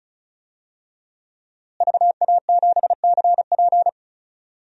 Morse code, heard as a keyed tone, about 36 wpm, 700 Hz. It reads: VA7CP
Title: DXCC-100 Call Signs - 5 Characters - Repeated 3x - 35wpm